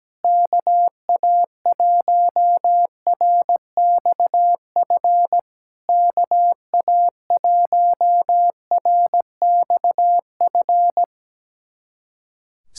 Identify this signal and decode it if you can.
Morse code, heard as a keyed tone, about 17 wpm, 690 Hz. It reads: KA1RXF KA1RXF
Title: SOTA QSO Practice 17wpm